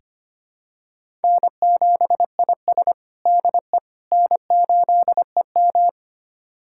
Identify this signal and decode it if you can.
Morse code, heard as a keyed tone, about 25 wpm, 700 Hz. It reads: N7IS DE N8EM
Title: USA Callsign DE Callsign - 4 Characters - 25wpm